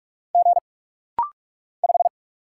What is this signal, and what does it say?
Morse code, heard as a keyed tone, about 46 wpm, 700 Hz. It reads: G 5
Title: Single Letter-Number Alt - Rapid-Fire 45wpm